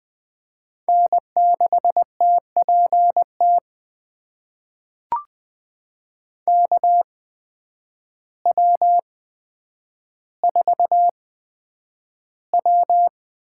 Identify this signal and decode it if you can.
Morse code, heard as a keyed tone, about 20 wpm, spaced as for 9 wpm, 700 Hz. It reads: N6TPT KW4W
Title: USA Call Signs - 5 Characters - 8x Character Spacing - 20wpm